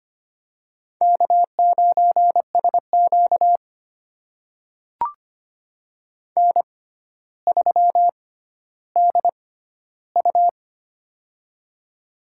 Morse code, heard as a keyed tone, about 25 wpm, spaced as for 13 wpm, 700 Hz. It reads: K9SQ N3DU
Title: USA Call Signs - 4 Characters - 6x Character Spacing - 25wpm